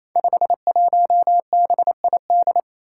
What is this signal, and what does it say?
Morse code, heard as a keyed tone, about 28 wpm, 700 Hz. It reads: 51BID
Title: Five Challenging Letter-Number Combinations 28wpm